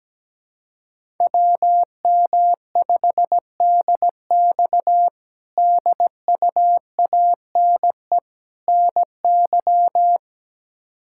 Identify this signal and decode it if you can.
Morse code, heard as a keyed tone, about 17 wpm, 700 Hz. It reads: WM5DX DUANE NY
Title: K1USN SST QSO Practice - 17wpm